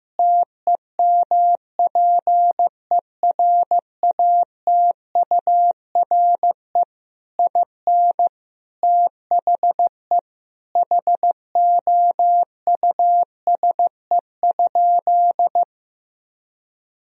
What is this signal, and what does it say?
Morse code, heard as a keyed tone, about 15 wpm, 700 Hz. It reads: TEMPERATURE IN THE HOUSE?
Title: Sentences from Top 1000 Words - Review Part 4 - 15wpm